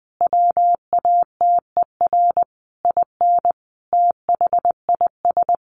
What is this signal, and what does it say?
Morse code, heard as a keyed tone, about 20 wpm, 700 Hz. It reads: WATER IN THIS
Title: Sentences from Top 1500 Words 20wpm